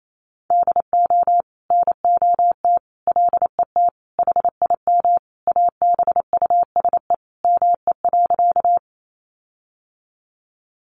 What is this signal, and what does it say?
Morse code, heard as a keyed tone, about 28 wpm, 700 Hz. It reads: DO NOT LET HIM ABUSE ME.
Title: Sentences from Top 1300 Words 28wpm